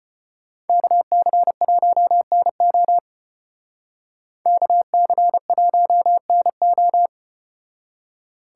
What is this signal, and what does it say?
Morse code, heard as a keyed tone, about 30 wpm, 700 Hz. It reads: KC1NO KC1NO
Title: Speed Racing - USA 2x2 Call Signs 25wpm